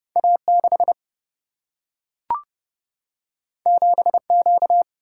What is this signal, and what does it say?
Morse code, heard as a keyed tone, about 30 wpm, 700 Hz. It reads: A 6 7 Q
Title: Single-Letter Single-Number 30wpm